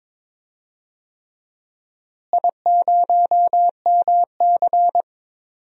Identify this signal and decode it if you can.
Morse code, heard as a keyed tone, about 21 wpm, 700 Hz. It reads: I0MC
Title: DXCC-100 Call Signs - 4 Characters - Repeated 3x - 22wpm